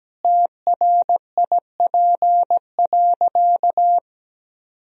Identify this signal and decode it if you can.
Morse code, heard as a keyed tone, about 17 wpm, 700 Hz. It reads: TRIP.
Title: Sentences from Top 1000 Words - Review Part 4 - 17wpm